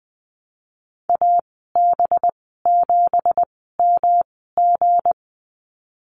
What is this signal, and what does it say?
Morse code, heard as a keyed tone, about 20 wpm, spaced as for 15 wpm, 700 Hz. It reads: AB7MG
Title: 245 Lesson 39g - Letter z - Call Signs 20wpm